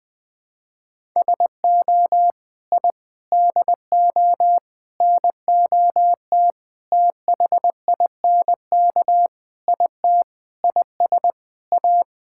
Morse code, heard as a keyed tone, about 20 wpm, 700 Hz. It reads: SO I DO NOT THINK IT IS A
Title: Sentences with 5-Word Phrases - 20wpm